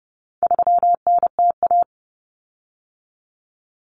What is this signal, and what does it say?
Morse code, heard as a keyed tone, about 30 wpm, 700 Hz. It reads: 3NTA
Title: Common Cut Numbers - 4-Digit 30wpm